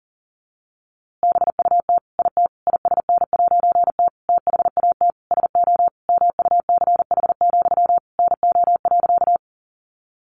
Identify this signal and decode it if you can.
Morse code, heard as a keyed tone, about 40 wpm, 700 Hz. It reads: BUT IT ISN'T THAT SO MUCH, NO.
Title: A Sound Of Thunder 40wpm